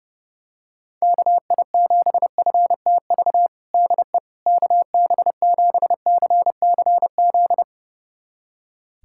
Morse code, heard as a keyed tone, about 32 wpm, 700 Hz. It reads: KI7FTV DE KB7CCZ
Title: USA Callsign DE Callsign - 6 Characters - 30wpm